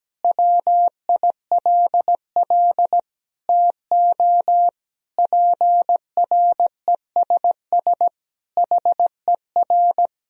Morse code, heard as a keyed tone, about 17 wpm, 700 Hz. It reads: WILL TO PRESS HER
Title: Sentences from Top 1000 Words - Review Part 1 - 17wpm